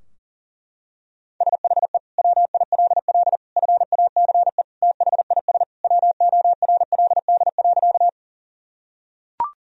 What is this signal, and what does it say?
Morse code, heard as a keyed tone, about 37 wpm, 700 Hz. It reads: SHE WILL FACE THIS WORLD.
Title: Sentences from Top 1000 Words - Review Part 2 - 40wpm